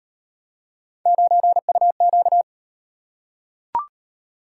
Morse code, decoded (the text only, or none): 9UQ